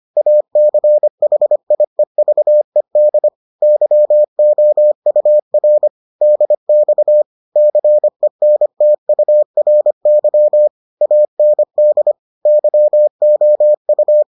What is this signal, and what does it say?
Morse code, heard as a keyed tone, about 25 wpm, 600 Hz. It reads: ACHIEVED YOUR DX CENTURY AND YOU